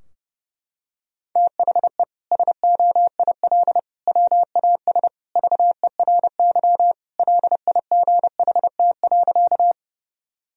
Morse code, heard as a keyed tone, about 32 wpm, 700 Hz. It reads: THE SOIL WAS VERY LIGHT.
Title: Sentences from Top 1000 Words - Review Part 4 - 30wpm